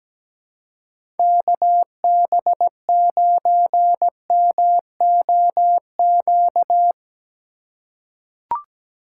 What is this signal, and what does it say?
Morse code, heard as a keyed tone, about 17 wpm, 700 Hz. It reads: KB9MOQ
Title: USA Call Signs - 6 Characters - 4x Character Spacing - 17wpm